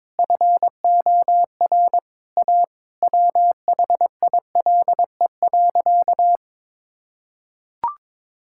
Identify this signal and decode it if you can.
Morse code, heard as a keyed tone, about 22 wpm, 700 Hz. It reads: FOR A WHILE.
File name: practice / 22wpm / Sentences with 3-Word Phrases - 22wpm